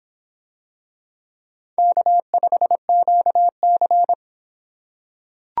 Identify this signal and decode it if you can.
Morse code, heard as a keyed tone, about 24 wpm, 700 Hz. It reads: K5QC